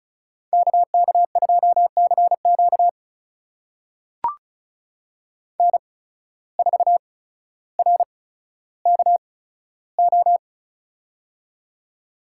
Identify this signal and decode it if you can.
Morse code, heard as a keyed tone, about 36 wpm, spaced as for 15 wpm, 700 Hz. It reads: KK2CQ N4RKO
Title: USA Call Signs - 5 Characters - 8x Character Spacing - 35wpm